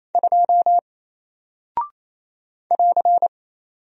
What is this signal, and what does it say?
Morse code, heard as a keyed tone, about 28 wpm, 700 Hz. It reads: 2 <AR>
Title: Single Character (Letter:Number:Punctuation:Prosign) Alt 28wpm